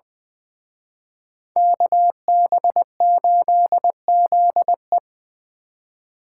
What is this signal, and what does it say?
Morse code, heard as a keyed tone, about 24 wpm, 700 Hz. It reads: KB8ZE
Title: Speed Racing - USA 2x2 Call Signs 20wpm